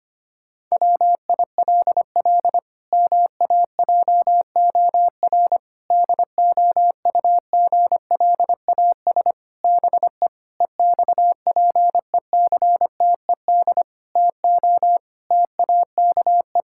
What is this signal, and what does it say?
Morse code, heard as a keyed tone, about 25 wpm, 700 Hz. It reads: WILL MAJOR DOUGLAS BE EXPECTED TO TAKE
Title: Pangram Challenge v2 - 25wpm